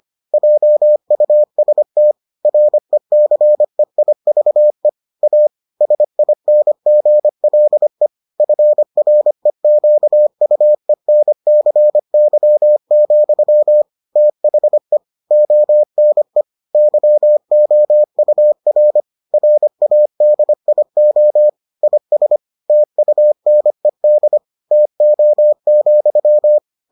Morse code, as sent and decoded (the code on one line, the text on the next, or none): .--- ..- ... - / .-. -.-. .. ...- . / .- / ... .. -. --. .-.. / ..-. .-. --.- ..- . -. -.-. -.-- --..-- / - .... / --- -. / -.-- --- ..- .-. / .-. .- -.. .. --- / .. ... / - ..- -. -.. / - --- --..--
JUST RECEIVE A SINGLE FREQUENCY, THE ONE YOUR RADIO IS TUNED TO,